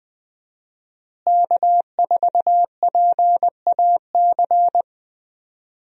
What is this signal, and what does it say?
Morse code, heard as a keyed tone, about 20 wpm, 700 Hz. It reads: K4PAC
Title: USA Call Signs - 5 Characters - Repeated 3x - 20wpm